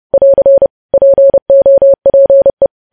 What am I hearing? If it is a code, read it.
Morse code, heard as a keyed tone, about 30 wpm, 550 Hz. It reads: <AR> POPE
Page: news headlines in morse code at 30 wpm